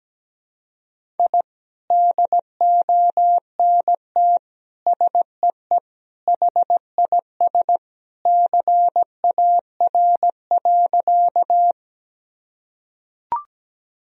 Morse code, heard as a keyed tone, about 17 wpm, 700 Hz. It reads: I DONT SEE HIS CAR.
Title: Sentences from Top 1000 Words - Review Part 1 - 17wpm